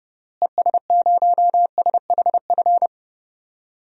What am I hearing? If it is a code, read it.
Morse code, heard as a keyed tone, about 32 wpm, 700 Hz. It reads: ES0SHF
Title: DXCC-100 Call Signs - 6 Characters - Repeated 3x - 30wpm